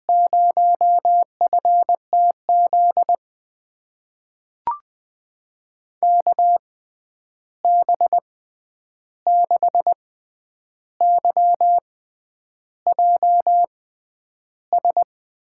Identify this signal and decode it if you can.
Morse code, heard as a keyed tone, about 20 wpm, spaced as for 9 wpm, 700 Hz. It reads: 0FTZ KB6YJS
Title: USA Call Signs - 6 Characters - 6x Character Spacing - 20wpm